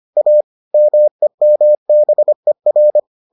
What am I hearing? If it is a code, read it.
Morse code, heard as a keyed tone, about 25 wpm, 600 Hz. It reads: A MEMBER